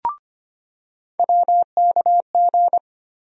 Morse code, heard as a keyed tone, about 25 wpm, 700 Hz. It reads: WKG
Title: USA Call Sign Postfixes - 3 Letters 25wpm